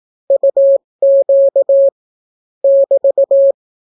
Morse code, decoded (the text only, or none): UQ <BT>